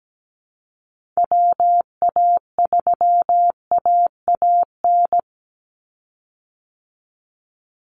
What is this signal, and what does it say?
Morse code, heard as a keyed tone, about 17 wpm, 700 Hz. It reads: WA3AAN
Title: Call Signs Repeated 2x - 17wpm